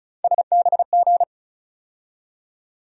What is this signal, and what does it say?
Morse code, heard as a keyed tone, about 36 wpm, 700 Hz. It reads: SBG
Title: USA Call Sign Postfixes - 3 Letters 35wpm